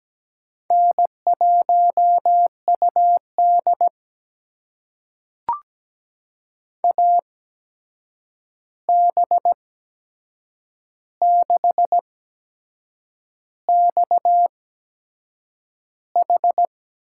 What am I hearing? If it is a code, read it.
Morse code, heard as a keyed tone, about 17 wpm, spaced as for 7 wpm, 700 Hz. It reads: N1UD AB6XH
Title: USA Call Signs - 8x Character Spacing - 17wpm